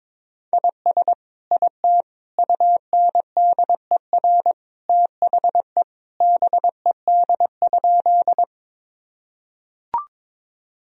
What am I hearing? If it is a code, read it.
Morse code, heard as a keyed tone, about 22 wpm, 700 Hz. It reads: IS IT UNDER THE BED?
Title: Sentences from Top 1000 Words - Review Part 1 - 22wpm